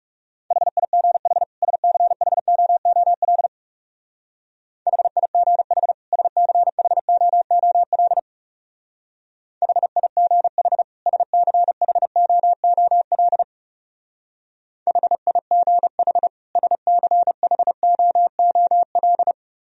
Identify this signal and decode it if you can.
Morse code, heard as a keyed tone, about 35 wpm, 700 Hz. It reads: HIGH SCHOOL HIGH SCHOOL HIGH SCHOOL HIGH SCHOOL
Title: Speed-Racing - Sets of 2 Words 30wpm